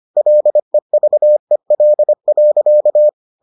Morse code, heard as a keyed tone, about 25 wpm, 600 Hz. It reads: LEVEL.